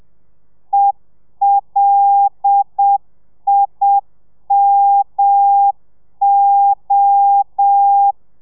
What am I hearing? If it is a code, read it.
Morse code, heard as a keyed tone, about 7 wpm, 800 Hz. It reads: ELIMO
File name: telegrafia / i2tfj / Lez5 30